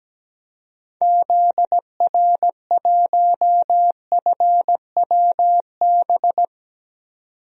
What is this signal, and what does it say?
Morse code, heard as a keyed tone, about 17 wpm, 700 Hz. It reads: ZR1FWB
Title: DXCC-100 Call Signs - 6 Characters - 17wpm